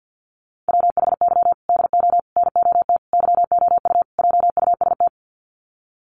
Morse code, heard as a keyed tone, about 51 wpm, 700 Hz. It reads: WHY DO NOT YOU JUST
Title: Sets of 5 Words 50wpm